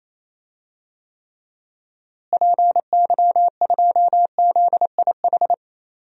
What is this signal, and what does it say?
Morse code, heard as a keyed tone, about 28 wpm, 700 Hz. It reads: PY2ZIH
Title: DXCC-100 Call Signs - 6 Characters - Repeated 3x - 28wpm